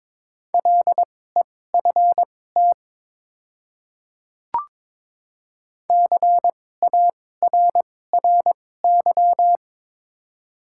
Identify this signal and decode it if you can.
Morse code, heard as a keyed tone, about 23 wpm, spaced as for 16 wpm, 700 Hz. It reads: LEFT CARRY
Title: 2x Character Spacing - Top 300 Words - 22wpm